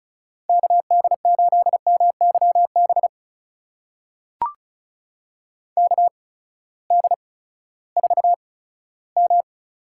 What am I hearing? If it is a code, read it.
Morse code, heard as a keyed tone, about 36 wpm, spaced as for 18 wpm, 700 Hz. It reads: KD8MYB KD4M
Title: USA Call Signs - 6 Characters - 8x Character Spacing - 35wpm